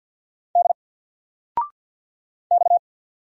Morse code, heard as a keyed tone, about 50 wpm, 700 Hz. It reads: D X
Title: Single Letter-Number 50wpm